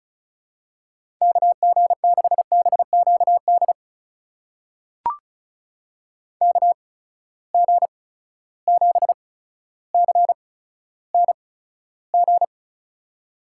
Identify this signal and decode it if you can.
Morse code, heard as a keyed tone, about 36 wpm, spaced as for 15 wpm, 700 Hz. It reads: KG6BQD KG7CNG
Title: USA Call Signs - 6 Characters - 8x Character Spacing - 35wpm